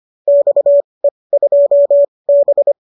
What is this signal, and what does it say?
Morse code, heard as a keyed tone, about 25 wpm, spaced as for 20 wpm, 570 Hz. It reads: XE2B